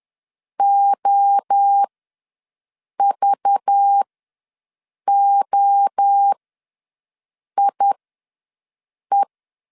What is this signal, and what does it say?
Morse code, heard as a keyed tone, about 10 wpm, spaced as for 5 wpm, 800 Hz. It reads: OVOIE
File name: LEZIONI / 5 LBVC / 5 LBV ws 10wpm 3645